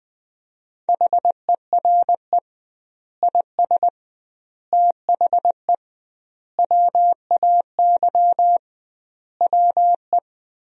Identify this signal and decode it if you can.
Morse code, heard as a keyed tone, about 20 wpm, 700 Hz. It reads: HERE IS THE WAY WE
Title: 2x Word-Spacing - Sentences from Top 200 Words - 20wpm